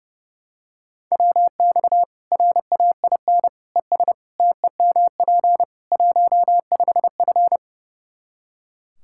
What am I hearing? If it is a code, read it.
Morse code, heard as a keyed tone, about 32 wpm, 700 Hz. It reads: WX RAIN ES TEMP 15F
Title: Practice QSO - Exchange 2 - Info - 30wpm